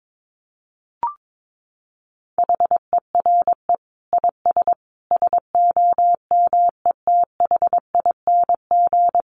Morse code, heard as a keyed tone, about 22 wpm, 700 Hz. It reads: HERE IS SOMETHING
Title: Sentences from Top 1000 Words 22wpm